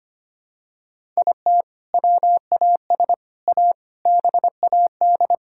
Morse code, heard as a keyed tone, about 25 wpm, 700 Hz. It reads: IT WAS A BAD
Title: Sentences from Top 1000 Words - Review Part 5 - 25wpm